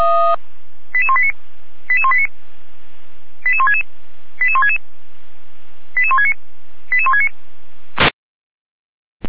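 Five-tone selective call (ZVEI), heard as quick runs of five tones, tones of about 80 ms. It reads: 88189 88189 88166 88166 88169 88169
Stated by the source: ZVEI five-tone selective call received on SDR.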